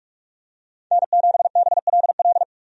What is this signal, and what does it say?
Morse code, heard as a keyed tone, about 46 wpm, 700 Hz. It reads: N7BLL
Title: USA Call Signs - 5 Characters - 45wpm